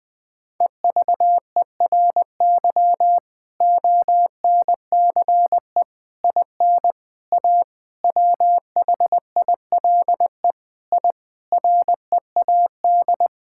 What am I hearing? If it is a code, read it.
Morse code, heard as a keyed tone, about 20 wpm, 700 Hz. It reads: EVERY ONCE IN A WHILE I READ
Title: Sentences with 5-Word Phrases - 20wpm